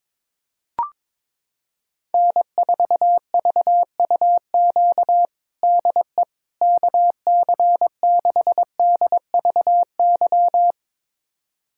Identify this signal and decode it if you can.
Morse code, heard as a keyed tone, about 22 wpm, 700 Hz. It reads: N4VUQ DE KC6DVY
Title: USA Callsign DE Callsign 22wpm